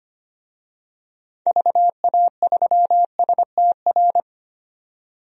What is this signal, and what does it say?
Morse code, heard as a keyed tone, about 25 wpm, 700 Hz. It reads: VA3STR
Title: DXCC-100 Call Signs - 6 Characters - 25wpm